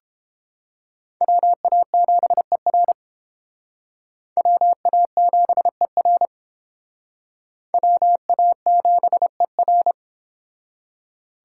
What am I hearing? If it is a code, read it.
Morse code, heard as a keyed tone, about 29 wpm, 700 Hz. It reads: WA7ER WA7ER WA7ER